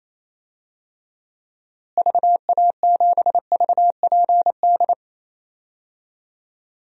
Morse code, heard as a keyed tone, about 28 wpm, 700 Hz. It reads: VA7VPD
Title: DXCC-100 Call Signs - 6 Characters - Repeated 3x - 28wpm